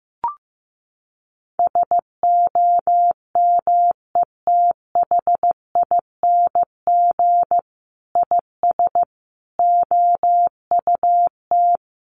Morse code, heard as a keyed tone, about 15 wpm, 700 Hz. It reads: SOMETHING IS OUT
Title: Sentences from Top 1400 Words 15wpm